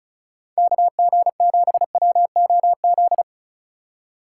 Morse code, heard as a keyed tone, about 34 wpm, 700 Hz. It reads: KG7WOZ
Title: USA Call Signs - 6 Characters - 2x Character Spacing - 35wpm